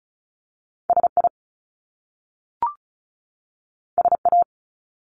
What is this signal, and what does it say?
Morse code, heard as a keyed tone, about 36 wpm, 700 Hz. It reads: SI SA
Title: Top Two Letter Combinations in Words 35wpm